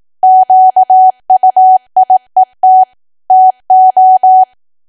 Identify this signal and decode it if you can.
Morse code, heard as a keyed tone, about 18 wpm, 750 Hz. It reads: QUIET TO
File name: cwa / W1AW-Level2 / 18WPM